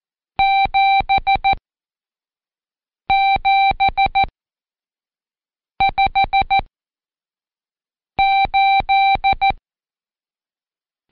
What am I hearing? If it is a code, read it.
Morse code, heard as a keyed tone, about 14 wpm, 780 Hz. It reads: 7 7 5 8
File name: Code Quick / AudioDownld1 / Seg82-Day9-Intro 8